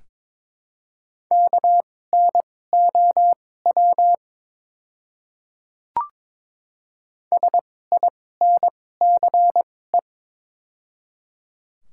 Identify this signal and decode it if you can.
Morse code, heard as a keyed tone, about 22 wpm, spaced as for 16 wpm, 700 Hz. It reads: KNOW SINCE
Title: 2x Character Spacing - Top 300 Words - 22wpm